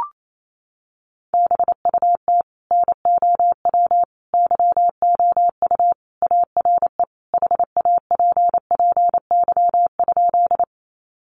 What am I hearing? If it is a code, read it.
Morse code, heard as a keyed tone, about 28 wpm, 700 Hz. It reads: BUT NOW YOU ARE HAPPY?
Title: Sentences from Top 1000 Words 28wpm